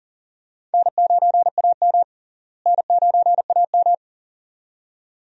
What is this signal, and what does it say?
Morse code, heard as a keyed tone, about 38 wpm, 700 Hz. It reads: N9AM N9AM
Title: USA Call Signs - 4 Characters - Repeated 2x - 40wpm